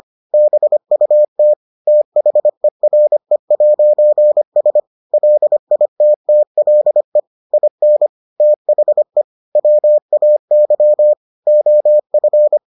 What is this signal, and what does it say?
Morse code, heard as a keyed tone, about 25 wpm, 600 Hz. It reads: BUT THERE'S LITTLE IN THE WAY OF